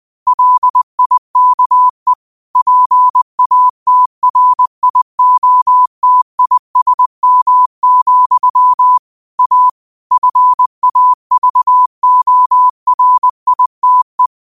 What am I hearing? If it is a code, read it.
Morse code, heard as a keyed tone, about 20 wpm, 1000 Hz. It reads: LIKE PATRIOTISM, A FAVORITE